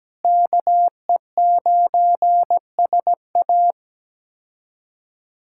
Morse code, heard as a keyed tone, about 17 wpm, 700 Hz. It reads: KE9SA
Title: K1USN SST QSO Practice - 17wpm